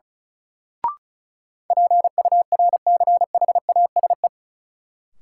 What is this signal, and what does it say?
Morse code, heard as a keyed tone, about 34 wpm, 700 Hz. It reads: PURCHASE
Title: Sentences from Top 800 Words - New Word Drill - 35wpm